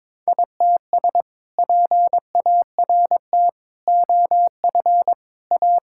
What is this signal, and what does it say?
Morse code, heard as a keyed tone, about 21 wpm, 700 Hz. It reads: ITS PART OF A
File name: practice / 22wpm / Sentences with 2-Word Phrases - 22wpm